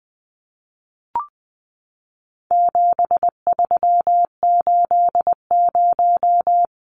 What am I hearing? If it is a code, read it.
Morse code, heard as a keyed tone, about 20 wpm, 700 Hz. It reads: 7380
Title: Numbers - 4-Digit 20wpm